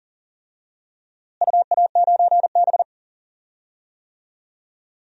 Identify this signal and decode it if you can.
Morse code, heard as a keyed tone, about 40 wpm, 700 Hz. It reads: UA9B
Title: DXCC-100 Call Signs - 4 Characters - Repeated 2x - 40wpm